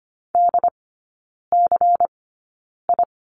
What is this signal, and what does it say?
Morse code, heard as a keyed tone, about 25 wpm, spaced as for 9 wpm, 700 Hz. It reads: DCI
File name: course / 25:10wpm / 093 Lesson 16e - Letter c - Three Characters 25wpm